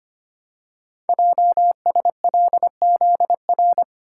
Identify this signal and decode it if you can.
Morse code, heard as a keyed tone, about 25 wpm, 700 Hz. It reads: JSLZR
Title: Code Groups - 5 Letters - 25wpm